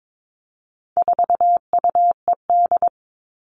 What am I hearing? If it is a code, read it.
Morse code, heard as a keyed tone, about 22 wpm, 700 Hz. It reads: 4UED
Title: All Cut Numbers - 4-Digit 22wpm